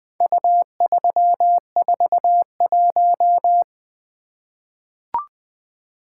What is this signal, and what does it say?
Morse code, heard as a keyed tone, about 20 wpm, 700 Hz. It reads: U341
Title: Four Letter-Number Combinations - 20wpm